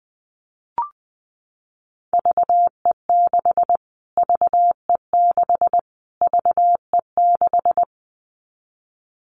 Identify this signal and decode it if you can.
Morse code, heard as a keyed tone, about 20 wpm, 700 Hz. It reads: VE6 VE6 VE6
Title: Callsign Prefixes for Top 25 DXCC - Repeated 3x - 20wpm